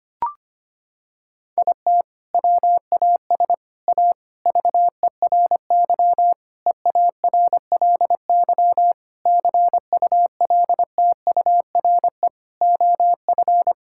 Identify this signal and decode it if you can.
Morse code, heard as a keyed tone, about 25 wpm, 700 Hz. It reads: IT WAS A VERY EARLY CULTURE OF
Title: Sentences from Top 1000 Words - Review Part 6 - 25wpm